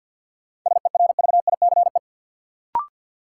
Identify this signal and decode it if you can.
Morse code, heard as a keyed tone, about 48 wpm, 700 Hz. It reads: SERVICE
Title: Sentences from Top 700 Words - New Word Drill - 50wpm